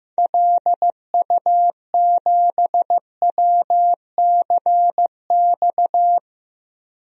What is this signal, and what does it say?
Morse code, heard as a keyed tone, about 15 wpm, 700 Hz. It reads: LU7WCX
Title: DXCC-100 Call Signs - 6 Characters - 15wpm